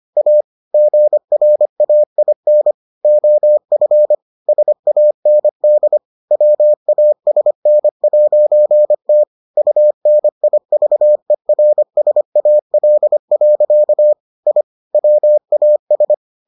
Morse code, heard as a keyed tone, about 25 wpm, 600 Hz. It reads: A GRAIN OF SAND WASN'T UNIVERSAL. I WAS